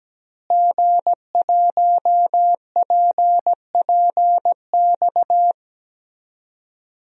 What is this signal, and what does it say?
Morse code, heard as a keyed tone, about 17 wpm, 700 Hz. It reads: G1PPX
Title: DXCC-100 Call Signs - 5 Characters - Repeated 2x - 17wpm